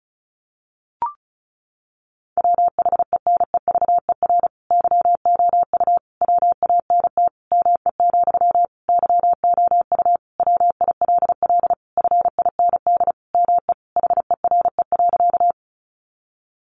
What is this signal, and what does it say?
Morse code, heard as a keyed tone, about 34 wpm, 700 Hz. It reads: WHENEVER YOU WANT ME, YOU WILL FIND ME HERE.
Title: Sentences from Top 1200 Words 35wpm